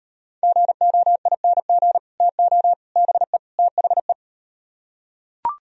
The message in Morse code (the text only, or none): GOING TO BE THE